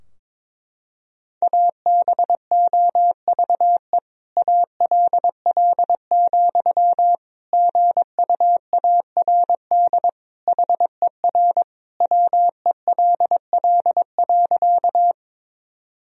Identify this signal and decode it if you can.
Morse code, heard as a keyed tone, about 22 wpm, 700 Hz. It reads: ABOVE ALL, GUARD HER WELL.
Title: Sentences from Top 1000 Words - Review Part 4 - 22wpm